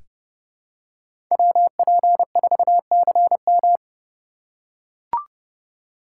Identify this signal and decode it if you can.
Morse code, heard as a keyed tone, about 30 wpm, 700 Hz. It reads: WP4CM
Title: USA Call Signs - 4x Character Spacing - 30wpm